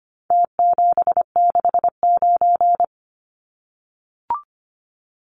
Morse code, heard as a keyed tone, about 25 wpm, 700 Hz. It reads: T769
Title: Numbers - Contest Style - 4-Digit 25wpm